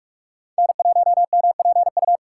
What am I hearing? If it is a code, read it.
Morse code, heard as a keyed tone, about 45 wpm, 700 Hz. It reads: N1MPU
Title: USA Call Signs - 5 Characters - 3x Character Spacing - 45wpm